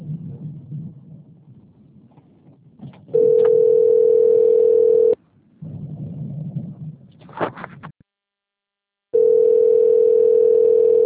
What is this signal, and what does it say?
Ringback tone (North America), 440 + 480 Hz.